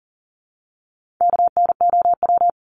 Morse code, heard as a keyed tone, about 40 wpm, 700 Hz. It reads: KNOW
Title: Top 300 Words Repeated 40wpm